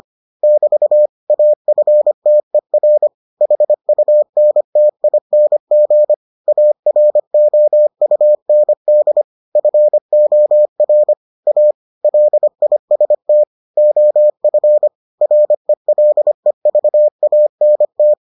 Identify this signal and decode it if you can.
Morse code, heard as a keyed tone, about 25 wpm, 600 Hz. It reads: <BT> AFTER HUNTING AROUND FOR A LIST OF RELEVANT